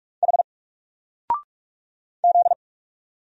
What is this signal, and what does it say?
Morse code, heard as a keyed tone, about 44 wpm, 700 Hz. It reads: H Z
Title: Single Character (Letter:Number:Punctuation:Prosign) Alt 45wpm